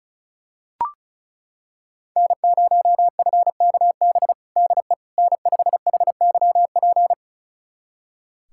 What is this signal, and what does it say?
Morse code, heard as a keyed tone, about 36 wpm, 700 Hz. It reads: N0FKB DE N5HYP
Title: USA Callsign DE Callsign - 5 Characters - 35wpm